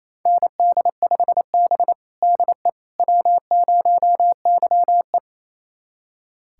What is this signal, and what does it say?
Morse code, heard as a keyed tone, about 28 wpm, 700 Hz. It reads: ND5B DE W0YE
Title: USA Callsign DE Callsign - 4 Characters - 28wpm